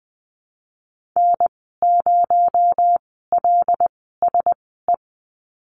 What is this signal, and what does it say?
Morse code, heard as a keyed tone, about 20 wpm, spaced as for 15 wpm, 700 Hz. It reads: N0LSE
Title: 231 Lesson 37g - Letter x - Call Signs 20wpm